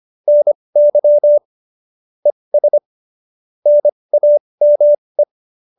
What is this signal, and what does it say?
Morse code, heard as a keyed tone, about 25 wpm, spaced as for 20 wpm, 590 Hz. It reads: NY ES NAME
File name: QSO302 20